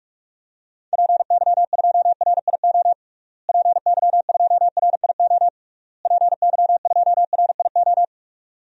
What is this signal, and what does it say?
Morse code, heard as a keyed tone, about 47 wpm, 700 Hz. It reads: PY2RIO PY2RIO PY2RIO
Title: DXCC-100 Call Signs - 6 Characters - Repeated 3x - 45wpm